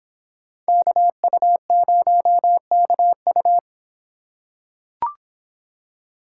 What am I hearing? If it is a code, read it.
Morse code, heard as a keyed tone, about 24 wpm, 700 Hz. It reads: KU0KU